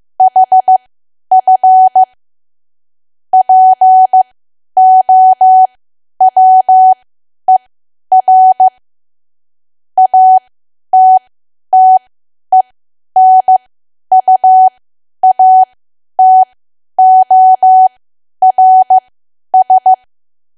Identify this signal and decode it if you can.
Morse code, heard as a keyed tone, about 15 wpm, spaced as for 10 wpm, 750 Hz. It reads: HF POWER ATTENUATORS
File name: cwa / W1AW-Level2 / 10WPM